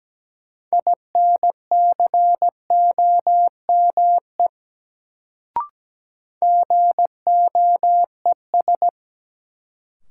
Morse code, heard as a keyed tone, about 17 wpm, 700 Hz. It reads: INCOME GOES
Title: Sentences from Top 1000 Words - Review Part 5 - New Word Drill - 17wpm